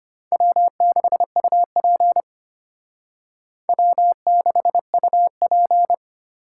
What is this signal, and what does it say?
Morse code, heard as a keyed tone, about 32 wpm, 700 Hz. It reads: W6UP W6UP